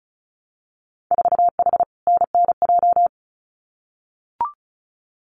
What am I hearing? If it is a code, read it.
Morse code, heard as a keyed tone, about 36 wpm, 700 Hz. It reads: 4H NNJ
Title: Winter Field Day Exchanges Encore 35wpm